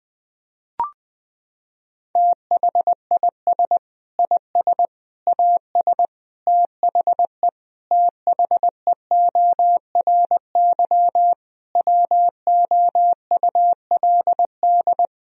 Morse code, heard as a keyed tone, about 21 wpm, 700 Hz. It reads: THIS IS AS THE THEORY WOULD
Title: Sentences from Top 1000 Words - Review Part 3 - 20wpm